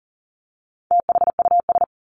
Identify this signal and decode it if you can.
Morse code, heard as a keyed tone, about 40 wpm, 700 Hz. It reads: THUS
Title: Top 751-1000 Words 40wpm